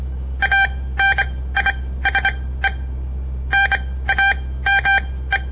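Morse code, heard as a keyed tone, about 25 wpm, spaced as for 16 wpm, 1600 Hz. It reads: ANISE NAME
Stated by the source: On-air Morse code practice broadcast by a southern Montana amateur radio association; all code, no read back.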